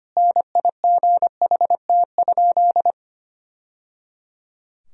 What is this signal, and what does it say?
Morse code, heard as a keyed tone, about 25 wpm, 700 Hz. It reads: NIGHT?
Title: Sentences from Top 1000 Words - Review Part 2 - 25wpm